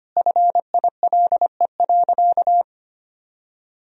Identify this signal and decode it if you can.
Morse code, heard as a keyed tone, about 25 wpm, 700 Hz. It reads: FILE.
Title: Sentences from Top 1000 Words - Review Part 4 - 25wpm